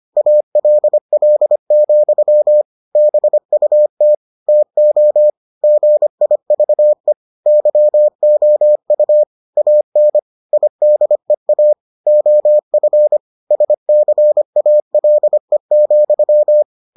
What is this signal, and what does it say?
Morse code, heard as a keyed tone, about 25 wpm, 600 Hz. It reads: ALL, BUT TO GIVE YOU AN IDEA OF SCALE,